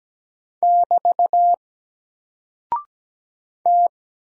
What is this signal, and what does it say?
Morse code, heard as a keyed tone, about 17 wpm, 700 Hz. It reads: <BT> T
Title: Single Character (Letter:Number:Punctuation:Prosign) Alt 17wpm